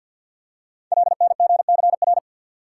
Morse code, heard as a keyed tone, about 51 wpm, 700 Hz. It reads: LNGCR
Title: Code Groups - 5 Letters - 50wpm